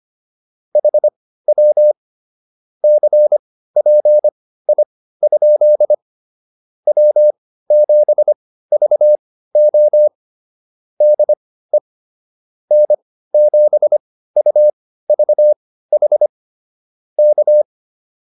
Morse code, heard as a keyed tone, about 25 wpm, spaced as for 15 wpm, 600 Hz. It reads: HW CPI? W7VO DE N7UVH K